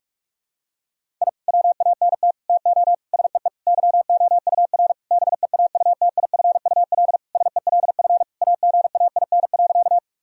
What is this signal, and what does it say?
Morse code, heard as a keyed tone, about 46 wpm, 700 Hz. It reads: I WANT TO SEE YOUR BEAUTIFUL SELF AGAIN.
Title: Sentences from Top 1000 Words - Review Part 1 - 45wpm